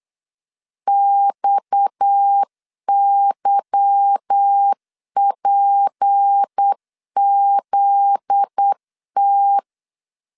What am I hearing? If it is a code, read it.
Morse code, heard as a keyed tone, about 8 wpm, 800 Hz. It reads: XYPZT
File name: LEZIONI / 8 YJ45 / 8 YJ 8wpm 3313